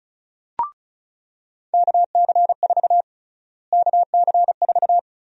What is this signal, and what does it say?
Morse code, heard as a keyed tone, about 36 wpm, 700 Hz. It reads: KC4 KC4
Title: USA Call Sign Prefixes - Repeated 2x - 35wpm